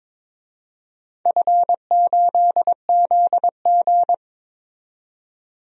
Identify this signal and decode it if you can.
Morse code, heard as a keyed tone, about 22 wpm, 700 Hz. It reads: F8ZG
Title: DXCC-100 Call Signs - 4 Characters - 22wpm